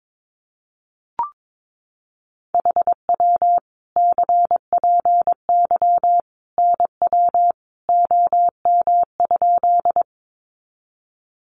Morse code, heard as a keyed tone, about 22 wpm, 700 Hz. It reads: HW CPY NW OM?
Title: Common QSO Phrases 22wpm